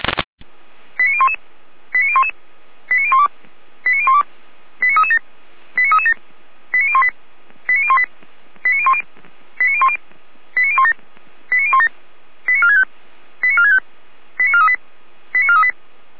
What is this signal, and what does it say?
Five-tone selective call (ZVEI), heard as quick runs of five tones, tones of about 70 ms. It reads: 89011 89011 89012 89012 89337 89337 89018 89018 89010 89010 89017 89017 89465 89465 89438 89438